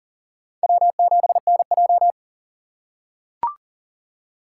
Morse code, heard as a keyed tone, about 40 wpm, 700 Hz. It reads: W7NJ
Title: USA Call Signs - 4 Characters - 4x Character Spacing - 40wpm